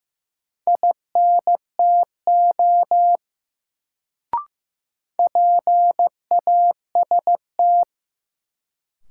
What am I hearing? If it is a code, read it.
Morse code, heard as a keyed tone, about 15 wpm, 700 Hz. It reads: INTO PAST
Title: Sets of 2 Words - New Word Drill - 15wpm